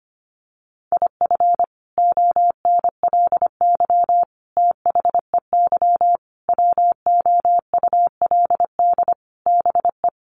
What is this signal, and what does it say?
Morse code, heard as a keyed tone, about 25 wpm, 700 Hz. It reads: IF ONLY THEY WOULD BE
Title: Sentences from Top 500 Words 25wpm